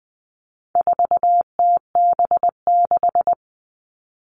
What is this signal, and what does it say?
Morse code, heard as a keyed tone, about 20 wpm, 700 Hz. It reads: 4TB6
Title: All Cut Numbers - 4-Digit 20wpm